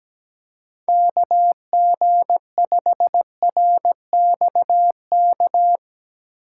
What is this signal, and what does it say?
Morse code, heard as a keyed tone, about 17 wpm, 700 Hz. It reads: KG5RXK
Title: USA Call Signs - 6 Characters - 2x Character Spacing - 17wpm